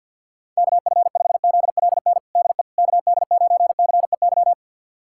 Morse code, heard as a keyed tone, about 52 wpm, 700 Hz. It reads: KF5ZLN DE KD9CEY
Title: USA Callsign DE Callsign 50wpm